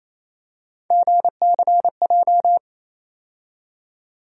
Morse code, heard as a keyed tone, about 27 wpm, 700 Hz. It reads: GCJ
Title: USA Call Sign Postfixes - 3 Letters 28wpm